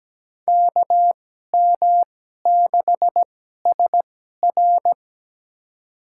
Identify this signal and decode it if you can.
Morse code, heard as a keyed tone, about 17 wpm, spaced as for 12 wpm, 700 Hz. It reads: KM6SR
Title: USA Call Signs - 5 Characters - 2x Character Spacing - 17wpm